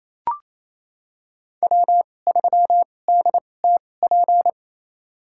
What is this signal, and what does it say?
Morse code, heard as a keyed tone, about 28 wpm, spaced as for 20 wpm, 700 Hz. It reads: W3DTP
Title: USA Call Signs - 5 Characters - 2x Character Spacing - 28wpm